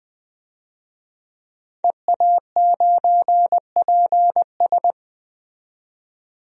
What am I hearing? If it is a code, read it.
Morse code, heard as a keyed tone, about 20 wpm, 700 Hz. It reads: EA9PS
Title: DXCC-100 Call Signs - 5 Characters - 20wpm